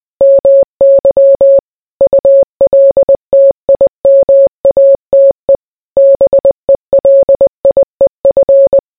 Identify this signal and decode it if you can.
Morse code, heard as a keyed tone, about 20 wpm, 550 Hz. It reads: MY ULTIMATE BELIEF